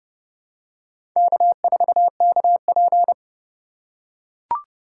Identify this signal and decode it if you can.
Morse code, heard as a keyed tone, about 30 wpm, 700 Hz. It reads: K4KP
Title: USA Call Signs - 4x Character Spacing - 30wpm